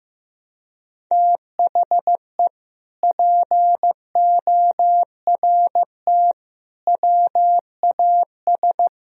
Morse code, heard as a keyed tone, about 15 wpm, 700 Hz. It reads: THE PORT WAS
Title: Sentences from Top 1000 Words - Review Part 1 - 15wpm